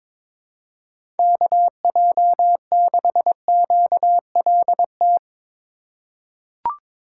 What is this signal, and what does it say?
Morse code, heard as a keyed tone, about 22 wpm, 700 Hz. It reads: KJ6QLT
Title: USA Call Signs - 6 Characters - 3x Character Spacing - 22wpm